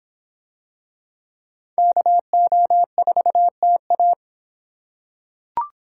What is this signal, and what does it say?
Morse code, heard as a keyed tone, about 24 wpm, 700 Hz. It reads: KO4TA